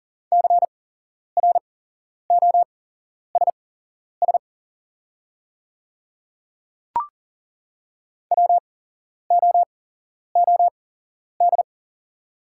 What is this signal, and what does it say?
Morse code, heard as a keyed tone, about 40 wpm, spaced as for 11 wpm, 700 Hz. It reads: CROSS WOOD
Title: 8x Character Spacing - Top 300 Words - 40wpm